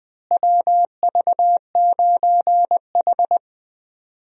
Morse code, heard as a keyed tone, about 20 wpm, 700 Hz. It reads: WV9H
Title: USA Call Signs - 4 Characters - Repeated 3x - 20wpm